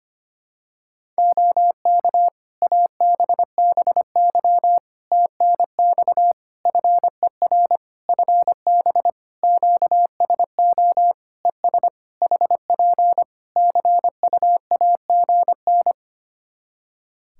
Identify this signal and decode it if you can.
Morse code, heard as a keyed tone, about 25 wpm, 700 Hz. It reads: OK ABBY TNX FER FB QSO ES HP CUAGN
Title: Practice QSO - Exchange 4 - Ending - 25wpm